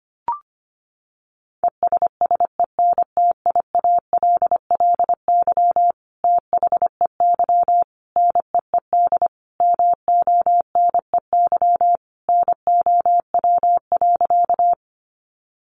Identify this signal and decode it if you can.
Morse code, heard as a keyed tone, about 25 wpm, 700 Hz. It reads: ESSENTIALLY THEY NEED MONEY NOW.
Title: Sentences from Top 1300 Words 25wpm